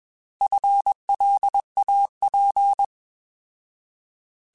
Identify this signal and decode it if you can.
Morse code, heard as a keyed tone, about 21 wpm, 800 Hz. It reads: FLAP